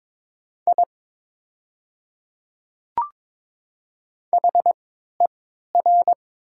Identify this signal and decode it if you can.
Morse code, heard as a keyed tone, about 22 wpm, spaced as for 12 wpm, 700 Hz. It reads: I HER
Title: 3x Character Spacing - Top 100 Words - 22wpm